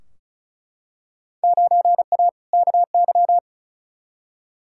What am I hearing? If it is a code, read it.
Morse code, heard as a keyed tone, about 36 wpm, 700 Hz. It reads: 9A KY
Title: Field Day Exchanges Encore - 35wpm